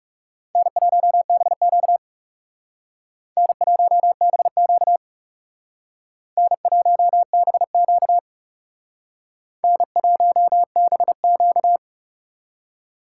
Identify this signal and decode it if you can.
Morse code, heard as a keyed tone, about 36 wpm, 700 Hz. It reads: N1BQ N1BQ N1BQ N1BQ
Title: Speed Racing - USA 1x2 Call Signs 30wpm